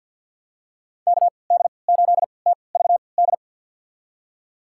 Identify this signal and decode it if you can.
Morse code, heard as a keyed tone, about 52 wpm, spaced as for 29 wpm, 700 Hz. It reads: KD8TVD
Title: USA Call Signs - 6 Characters - 3x Character Spacing - 50wpm